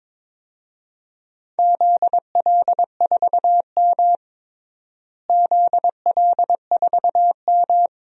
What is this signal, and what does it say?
Morse code, heard as a keyed tone, about 22 wpm, 700 Hz. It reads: ZL4M ZL4M
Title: DXCC-100 Call Signs - 4 Characters - Repeated 2x - 22wpm